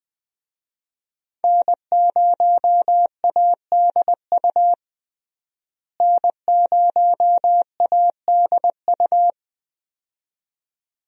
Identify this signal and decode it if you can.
Morse code, heard as a keyed tone, about 20 wpm, 700 Hz. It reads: N0ADU N0ADU
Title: USA Call Signs - 5 Characters - Repeated 2x - 20wpm